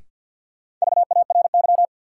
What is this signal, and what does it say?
Morse code, heard as a keyed tone, about 52 wpm, 700 Hz. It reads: VARY
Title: Sentences from Top 1200 Words - New Word Drill - 50wpm